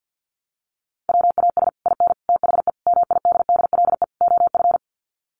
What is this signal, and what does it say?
Morse code, heard as a keyed tone, about 50 wpm, 700 Hz. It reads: WAS IN THE MIDDLE OF
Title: Sets of 5 Words Encore 50wpm